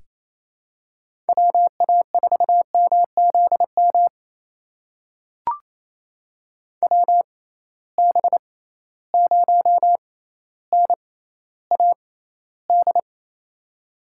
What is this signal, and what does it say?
Morse code, heard as a keyed tone, about 28 wpm, spaced as for 14 wpm, 700 Hz. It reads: WA4MZM WB0NAD
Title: USA Call Signs - 6 Characters - 6x Character Spacing - 28wpm